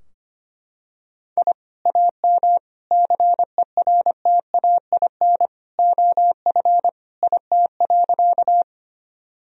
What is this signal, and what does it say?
Morse code, heard as a keyed tone, about 25 wpm, 700 Hz. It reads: I AM CERTAIN OF IT.
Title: Sentences from Top 1000 Words - Review Part 2 - 25wpm